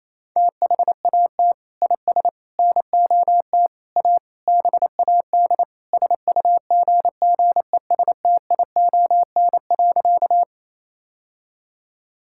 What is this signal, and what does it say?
Morse code, heard as a keyed tone, about 28 wpm, 700 Hz. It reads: THAT IS NOT A BAD SUGGESTION.
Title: Sentences from Top 1600 Words 28wpm